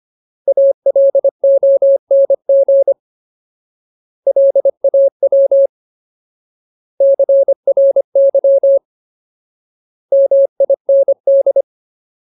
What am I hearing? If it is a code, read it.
Morse code, heard as a keyed tone, about 25 wpm, 550 Hz. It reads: ALONG LAW CRY MIND
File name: Ultimate Word List / 25WPM Ultimate